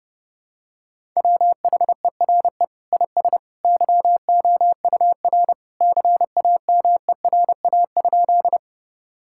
Morse code, heard as a keyed tone, about 30 wpm, 700 Hz. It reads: WHERE IS YOUR CAMERA?
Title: Sentences from Top 1000 Words - Review Part 5 - 30wpm